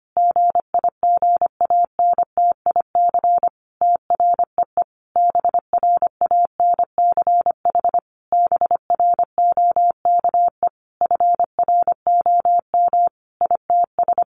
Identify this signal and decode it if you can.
Morse code, heard as a keyed tone, about 25 wpm, 700 Hz. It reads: GIGANTIC TREE BRANCH BROKE FROM ITS
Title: A Sound Of Thunder 25wpm